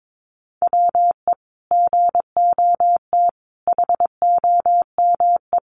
Morse code, heard as a keyed tone, about 22 wpm, 700 Hz. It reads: WE GOT HOME
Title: Sentences from Top 900 Words 22wpm